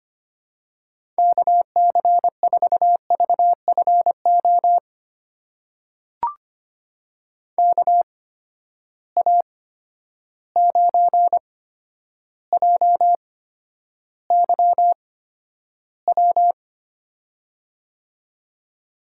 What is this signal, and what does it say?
Morse code, heard as a keyed tone, about 25 wpm, spaced as for 11 wpm, 700 Hz. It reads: KC4VFO KA9JYW
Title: USA Call Signs - 6 Characters - 8x Character Spacing - 25wpm